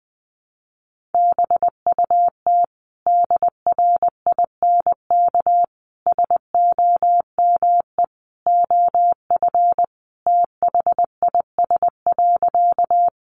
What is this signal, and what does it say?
Morse code, heard as a keyed tone, about 20 wpm, 700 Hz. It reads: BUT DRINK SOME OF THIS.
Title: Sentences from Top 1500 Words 20wpm